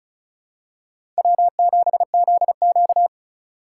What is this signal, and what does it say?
Morse code, heard as a keyed tone, about 36 wpm, 700 Hz. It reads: W7ZQ
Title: USA Call Signs - 8x Character Spacing - 35wpm